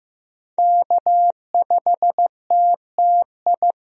Morse code, heard as a keyed tone, about 15 wpm, 700 Hz. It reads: K5TTI
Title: USA Call Signs - 5 Characters - Repeated 3x - 15wpm